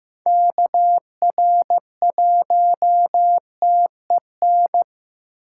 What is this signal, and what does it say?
Morse code, heard as a keyed tone, about 18 wpm, 700 Hz. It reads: KR1TEN
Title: Speed Racing - USA 2x3 Call Signs 15wpm